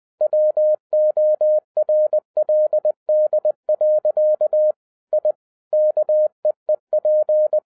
Morse code, heard as a keyed tone, about 20 wpm, 610 Hz. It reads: WORLD. I KEEP